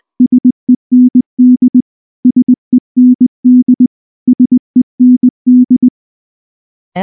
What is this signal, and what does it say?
Morse code, heard as a keyed tone, about 19 wpm, 250 Hz. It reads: SEND SEND SEND
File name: morse / morse-training-20WPM-10